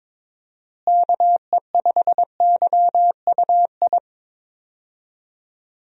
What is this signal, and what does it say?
Morse code, heard as a keyed tone, about 22 wpm, 700 Hz. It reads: KE5YUI
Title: USA Call Signs - 6 Characters - Repeated 2x - 22wpm